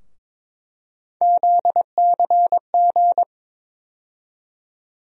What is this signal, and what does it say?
Morse code, heard as a keyed tone, about 22 wpm, 700 Hz. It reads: ZCG
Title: USA Call Sign Postfixes - 3 Letters 22wpm